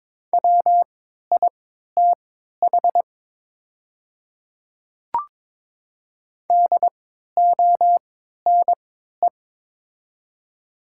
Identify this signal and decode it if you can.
Morse code, heard as a keyed tone, about 22 wpm, spaced as for 12 wpm, 700 Hz. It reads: WITH DONE
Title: 3x Character Spacing - Top 500 Words - 22wpm